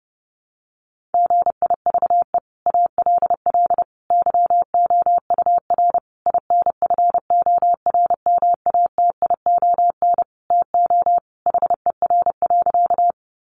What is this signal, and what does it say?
Morse code, heard as a keyed tone, about 28 wpm, 700 Hz. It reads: GIVE ALL YOUR INFORMATION TO HER.
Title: Sentences from Top 600 Words 30wpm